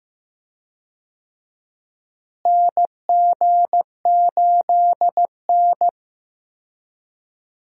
Morse code, heard as a keyed tone, about 15 wpm, 700 Hz. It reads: NG8N
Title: USA Call Signs - 4 Characters - Repeated 2x - 15wpm